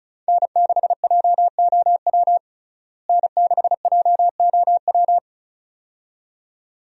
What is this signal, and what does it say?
Morse code, heard as a keyed tone, about 35 wpm, 700 Hz. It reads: N6JOW N6JOW
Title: USA Call Signs - 5 Characters - Repeated 2x - 35wpm